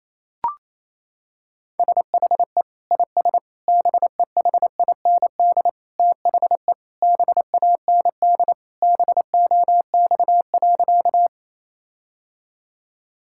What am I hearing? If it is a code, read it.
Morse code, heard as a keyed tone, about 28 wpm, 700 Hz. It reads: SHE IS BEHIND THE BAND BOX.
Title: Sentences from Top 1600 Words 28wpm